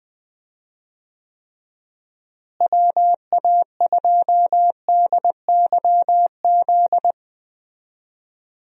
Morse code, heard as a keyed tone, about 20 wpm, 700 Hz. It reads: WA2DYZ
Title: USA Call Signs - 6 Characters - Repeated 3x - 20wpm